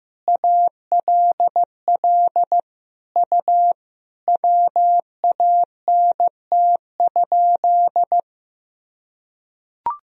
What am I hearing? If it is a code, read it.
Morse code, heard as a keyed tone, about 15 wpm, 700 Hz. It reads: ALL U WANT?
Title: Sentences from Top 200 Words and Abbreviations 15wpm